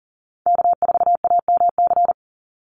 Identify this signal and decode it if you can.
Morse code, heard as a keyed tone, about 40 wpm, 700 Hz. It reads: K4AMC
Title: Call Signs Repeated 2x - 40wpm